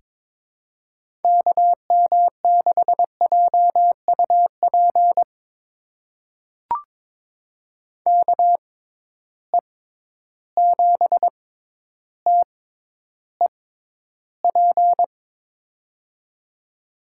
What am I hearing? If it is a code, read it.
Morse code, heard as a keyed tone, about 21 wpm, spaced as for 11 wpm, 700 Hz. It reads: KM6JUP KE7TEP
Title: USA Call Signs - 6 Characters - 6x Character Spacing - 22wpm